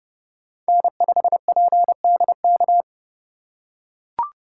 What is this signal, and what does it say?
Morse code, heard as a keyed tone, about 30 wpm, 700 Hz. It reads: N5PDK
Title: USA Call Signs - 5 Characters - 2x Character Spacing - 30wpm